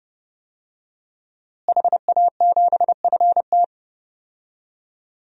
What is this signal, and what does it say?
Morse code, heard as a keyed tone, about 31 wpm, 700 Hz. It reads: HA7FT
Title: DXCC-100 Call Signs - 5 Characters - Repeated 3x - 30wpm